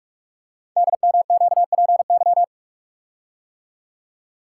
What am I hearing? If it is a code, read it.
Morse code, heard as a keyed tone, about 44 wpm, 700 Hz. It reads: DMQPY
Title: Code Groups - 5 Letters - 45wpm